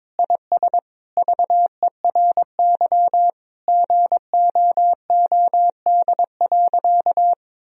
Morse code, heard as a keyed tone, about 22 wpm, 700 Hz. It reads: IS VERY GOOD.
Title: Sentences from Top 1000 Words - Review Part 1 - 22wpm